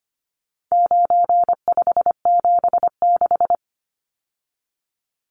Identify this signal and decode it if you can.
Morse code, heard as a keyed tone, about 25 wpm, 700 Hz. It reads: 9576
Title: Numbers - 4-Digit 25wpm